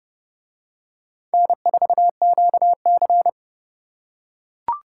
Morse code, heard as a keyed tone, about 28 wpm, 700 Hz. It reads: N4QC